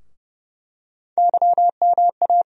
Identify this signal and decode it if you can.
Morse code, heard as a keyed tone, about 30 wpm, 700 Hz. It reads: YMA
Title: USA Call Sign Postfixes - 3 Letters 30wpm